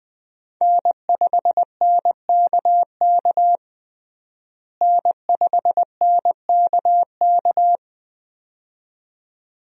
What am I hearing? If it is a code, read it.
Morse code, heard as a keyed tone, about 20 wpm, 700 Hz. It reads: N5NKK N5NKK
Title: USA Call Signs - 5 Characters - Repeated 2x - 20wpm